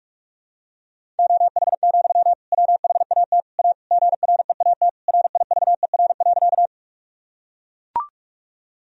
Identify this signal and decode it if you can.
Morse code, heard as a keyed tone, about 42 wpm, 700 Hz. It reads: OH, WHAT A GREAT RIVER.
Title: Sentences from Top 1000 Words - Review Part 1 - 45wpm